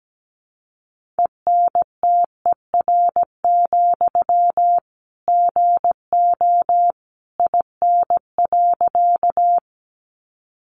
Morse code, heard as a keyed tone, about 17 wpm, 700 Hz. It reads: ENTER, GO IN.
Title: Sentences from Top 1300 Words 17wpm